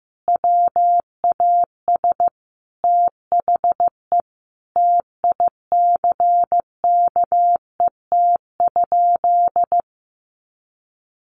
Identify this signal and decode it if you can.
Morse code, heard as a keyed tone, about 15 wpm, 700 Hz. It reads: WAS THE TICKET?
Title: Sentences from Top 1400 Words 15wpm